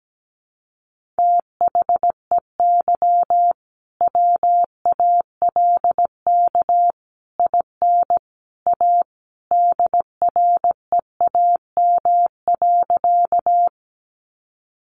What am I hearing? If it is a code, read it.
Morse code, heard as a keyed tone, about 17 wpm, 700 Hz. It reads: THEY WALK IN A DREAM.
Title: Sentences from Top 1500 Words 17wpm